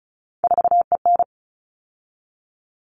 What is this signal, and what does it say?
Morse code, heard as a keyed tone, about 34 wpm, 700 Hz. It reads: 4EN
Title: All Cut Numbers - 3-Digit 35wpm